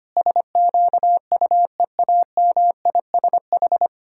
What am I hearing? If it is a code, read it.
Morse code, heard as a keyed tone, about 25 wpm, 700 Hz. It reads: SQUEAMISH
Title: Pangram Challenge v2 - 25wpm